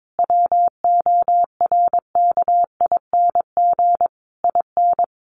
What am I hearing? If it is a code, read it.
Morse code, heard as a keyed tone, about 22 wpm, 700 Hz. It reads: WORKING IN
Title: Sentences from Top 1200 Words 22wpm